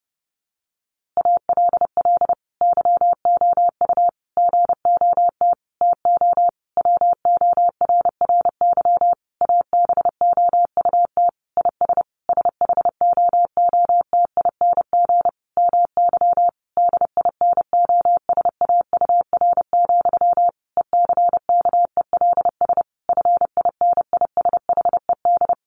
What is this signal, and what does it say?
Morse code, heard as a keyed tone, about 30 wpm, 700 Hz. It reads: ALL YOU GOT TO WORRY ABOUT IS SHOOTING MY DINOSAUR, ECKELS FINISHED
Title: A Sound Of Thunder 30wpm